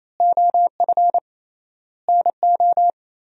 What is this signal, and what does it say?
Morse code, heard as a keyed tone, about 28 wpm, 700 Hz. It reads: OF NO
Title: 3x Word-Spacing - Sentences from Top 100 Words - 28wpm